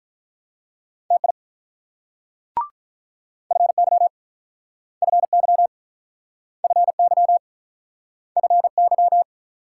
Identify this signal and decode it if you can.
Morse code, heard as a keyed tone, about 44 wpm, 690 Hz. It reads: TI FY FY FY FY